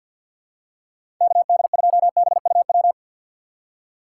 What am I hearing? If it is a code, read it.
Morse code, heard as a keyed tone, about 52 wpm, 700 Hz. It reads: KD2BUW
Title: USA Call Signs - 6 Characters - 6x Character Spacing - 50wpm